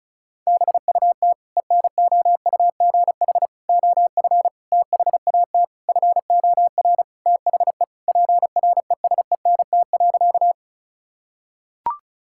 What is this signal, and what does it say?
Morse code, heard as a keyed tone, about 36 wpm, 700 Hz. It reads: BUT ENOUGH OF THAT FOR THE PRESENT.
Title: Sentences from Top 1000 Words - Review Part 5 - 35wpm